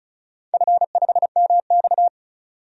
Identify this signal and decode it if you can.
Morse code, heard as a keyed tone, about 36 wpm, 700 Hz. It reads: F5MX
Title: DXCC-100 Call Signs - 4 Characters - 35wpm